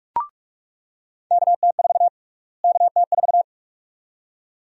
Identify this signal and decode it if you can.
Morse code, heard as a keyed tone, about 47 wpm, 700 Hz. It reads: KT4 KT4
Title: USA Call Sign Prefixes - Repeated 2x - 45wpm